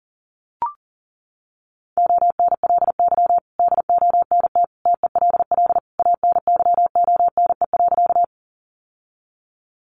Morse code, heard as a keyed tone, about 40 wpm, 700 Hz. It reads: ONLY DONT TELL ANYONE.
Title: Sentences from Top 900 Words 40wpm